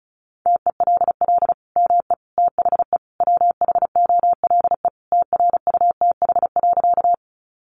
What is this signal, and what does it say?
Morse code, heard as a keyed tone, about 34 wpm, 700 Hz. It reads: TELL ME THE WHOLE TRUTH.
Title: Sentences from Top 800 Words 35wpm